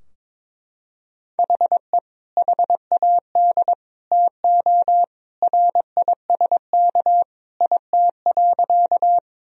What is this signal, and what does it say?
Morse code, heard as a keyed tone, about 22 wpm, 700 Hz. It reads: HE HAD TO RISK IT.
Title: Sentences from Top 1000 Words - Review Part 4 - 22wpm